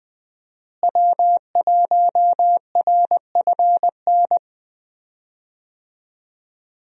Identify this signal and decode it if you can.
Morse code, heard as a keyed tone, about 20 wpm, 700 Hz. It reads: W1RFN
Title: USA Call Signs - 5 Characters - Repeated 3x - 20wpm